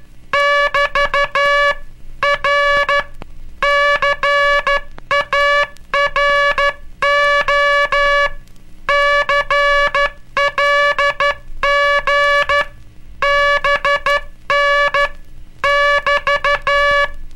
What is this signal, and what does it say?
Morse code, heard as a keyed tone, about 12 wpm, 1700 Hz. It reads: <BT> R CARO CLG BN <BT>
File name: cursos / Curso CW mp3 / CW Aula - 18